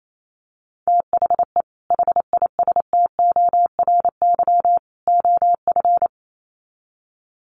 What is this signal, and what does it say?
Morse code, heard as a keyed tone, about 28 wpm, 700 Hz. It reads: THE HISTORY OF
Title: Sets of 3 Words 28wpm